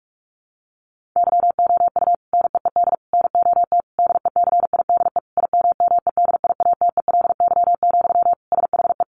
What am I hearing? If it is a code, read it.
Morse code, heard as a keyed tone, about 44 wpm, 700 Hz. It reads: YOU NEED NOT DECIDE IMMEDIATELY, SHE
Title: Sentences from Top 1500 Words 45wpm